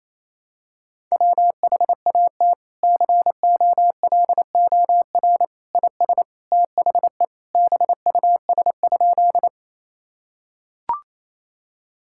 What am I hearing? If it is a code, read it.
Morse code, heard as a keyed tone, about 28 wpm, 700 Hz. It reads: WHAT COLOR IS THE BUS?
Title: Sentences from Top 1000 Words - Review Part 6 - 28wpm